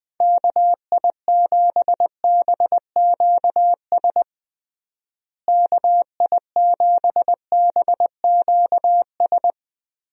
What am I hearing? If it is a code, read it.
Morse code, heard as a keyed tone, about 20 wpm, 700 Hz. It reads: KI7BQS KI7BQS
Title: USA Call Signs - 6 Characters - Repeated 3x - 20wpm